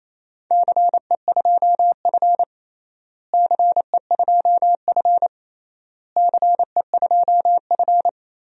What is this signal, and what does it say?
Morse code, heard as a keyed tone, about 28 wpm, 700 Hz. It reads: CE2F CE2F CE2F
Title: DXCC-100 Call Signs - 4 Characters - Repeated 3x - 28wpm